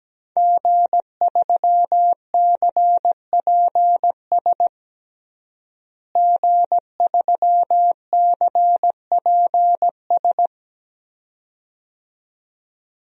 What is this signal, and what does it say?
Morse code, heard as a keyed tone, about 17 wpm, 700 Hz. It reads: G3CPS G3CPS
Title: DXCC-100 Call Signs - 5 Characters - Repeated 2x - 17wpm